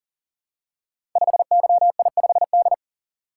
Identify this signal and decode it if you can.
Morse code, heard as a keyed tone, about 40 wpm, 690 Hz. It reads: 5YI5D
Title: Code Groups - 5 Letter:Numbers - 40wpm